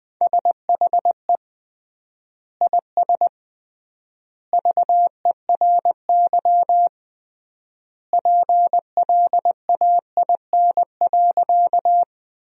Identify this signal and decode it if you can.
Morse code, heard as a keyed tone, about 20 wpm, 700 Hz. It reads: SHE IS VERY PLAIN.
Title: 3x Word-Spacing - Sentences from Top 400 Words - 20wpm